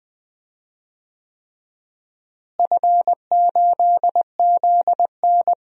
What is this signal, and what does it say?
Morse code, heard as a keyed tone, about 20 wpm, 700 Hz. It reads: F8ZN
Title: DXCC-100 Call Signs - 4 Characters - Repeated 3x - 20wpm